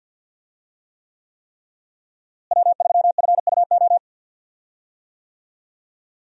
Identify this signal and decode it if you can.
Morse code, heard as a keyed tone, about 46 wpm, 700 Hz. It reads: W3FUO
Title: USA Call Signs - 5 Characters - Repeated 3x - 50wpm